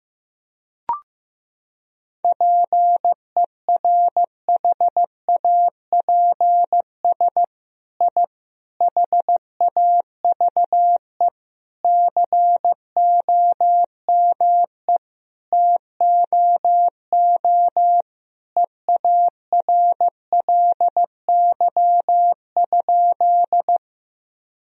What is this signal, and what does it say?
Morse code, heard as a keyed tone, about 15 wpm, 700 Hz. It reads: PERHAPS I HAVE COME TOO EARLY?
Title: Sentences from Top 1000 Words - Review Part 2 - 15wpm